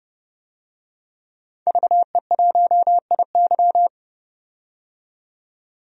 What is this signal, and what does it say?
Morse code, heard as a keyed tone, about 30 wpm, 700 Hz. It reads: VE1IY
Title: DXCC-100 Call Signs - 5 Characters - 30wpm